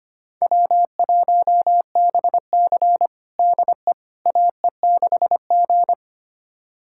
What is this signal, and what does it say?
Morse code, heard as a keyed tone, about 25 wpm, 700 Hz. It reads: W1BC DE AE6G
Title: USA Callsign DE Callsign - 4 Characters - 25wpm